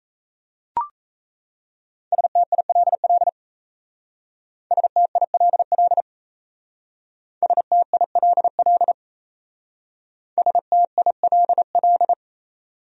Morse code, heard as a keyed tone, about 35 wpm, 700 Hz. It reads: STILL STILL STILL STILL
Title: Speed-Racing Top 300 Words 28wpm